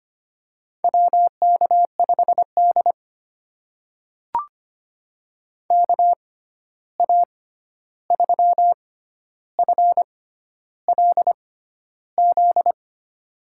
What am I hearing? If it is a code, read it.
Morse code, heard as a keyed tone, about 25 wpm, spaced as for 11 wpm, 700 Hz. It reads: WK5D KA3FLZ
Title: USA Call Signs - 6x Character Spacing - 25wpm